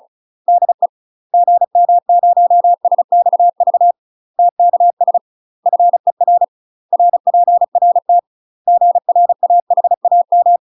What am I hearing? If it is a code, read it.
Morse code, heard as a keyed tone, about 36 wpm, 700 Hz. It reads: DE GM0SXV TKS FER RPRT GRAHAM
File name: UK MASTER QSOs TRAINING / 35WPM / UK MASTER QSOs 0034